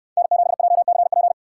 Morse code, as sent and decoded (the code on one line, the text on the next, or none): -. --... .--- -.-. .--.
N7JCP